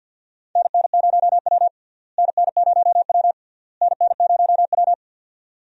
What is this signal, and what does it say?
Morse code, heard as a keyed tone, about 51 wpm, 700 Hz. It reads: NN0W NN0W NN0W
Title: USA Call Signs - 4 Characters - Repeated 3x - 50wpm